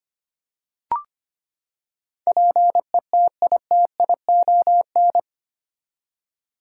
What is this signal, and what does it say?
Morse code, heard as a keyed tone, about 25 wpm, 700 Hz. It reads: PETITION
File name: Eight Letter Words - 25wpm